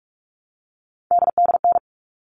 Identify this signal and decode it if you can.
Morse code, heard as a keyed tone, about 44 wpm, 700 Hz. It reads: DDN
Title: All Cut Numbers - 3-Digit 45wpm